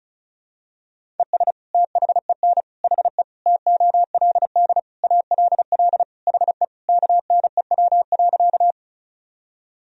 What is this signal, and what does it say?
Morse code, heard as a keyed tone, about 36 wpm, 700 Hz. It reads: ES THEN HE TOLD ALL HE KNEW.
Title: Sentences from Top 400 Words and Abbreviations 35wpm